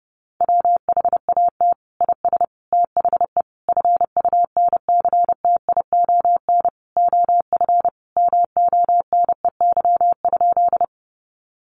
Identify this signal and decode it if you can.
Morse code, heard as a keyed tone, about 30 wpm, 700 Hz. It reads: WHAT IS THE FUNCTION OF MONEY?
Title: Sentences from Top 1100 Words 30wpm